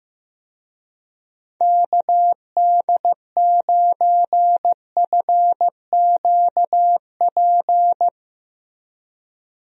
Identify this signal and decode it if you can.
Morse code, heard as a keyed tone, about 15 wpm, 700 Hz. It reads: KD9FQP
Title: USA Call Signs - 6 Characters - Repeated 2x - 15wpm